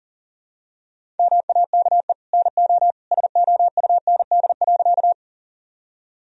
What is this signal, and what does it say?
Morse code, heard as a keyed tone, about 37 wpm, 700 Hz. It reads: MAKE NO SOUND.
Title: Sentences from Top 1000 Words - Review Part 1 - 40wpm